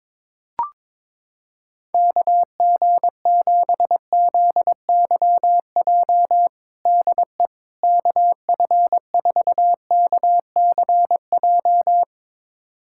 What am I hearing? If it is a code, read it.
Morse code, heard as a keyed tone, about 21 wpm, 700 Hz. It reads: KG7ZYJ DE KF4KCJ
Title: USA Callsign DE Callsign - 6 Characters - 22wpm